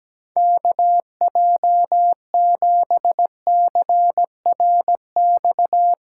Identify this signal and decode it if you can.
Morse code, heard as a keyed tone, about 17 wpm, 700 Hz. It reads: KJ7CRX
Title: USA Callsign DE Callsign 17wpm